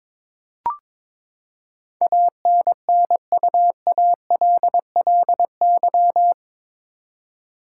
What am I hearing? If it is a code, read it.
Morse code, heard as a keyed tone, about 22 wpm, 700 Hz. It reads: ANNUALLY
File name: practice / 22wpm / Eight Letter Words - 22wpm